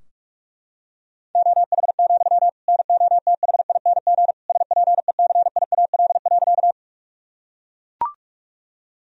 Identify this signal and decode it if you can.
Morse code, heard as a keyed tone, about 41 wpm, 700 Hz. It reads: OH, NOTHING SPECIAL.
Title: Sentences from Top 1000 Words - Review Part 3 - 45wpm